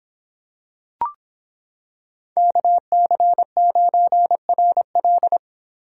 Morse code, heard as a keyed tone, about 26 wpm, 700 Hz. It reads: KC9RL